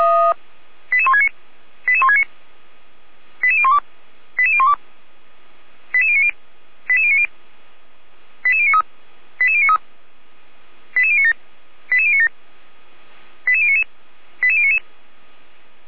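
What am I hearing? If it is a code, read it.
Five-tone selective call (ZVEI), heard as quick runs of five tones, tones of about 70 ms. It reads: 88169 88169 88012 88012 88090 88090 88093 88093 88097 88097 88099 88099
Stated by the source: ZVEI five-tone selective call received by SDR, from the audio server of Feuerwehr Orlamünde.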